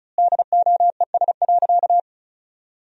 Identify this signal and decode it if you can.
Morse code, heard as a keyed tone, about 34 wpm, 700 Hz. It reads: DOES.
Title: 4x Word-Spacing - Sentences from Top 500 Words - 35wpm